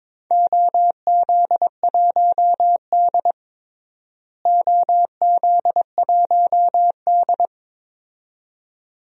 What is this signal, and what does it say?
Morse code, heard as a keyed tone, about 22 wpm, 700 Hz. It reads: OZ1D OZ1D
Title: DXCC-100 Call Signs - 4 Characters - Repeated 2x - 22wpm